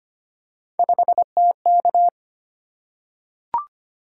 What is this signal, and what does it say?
Morse code, heard as a keyed tone, about 25 wpm, 700 Hz. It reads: 5TK
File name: Three Letter-Number Combinations - 25wpm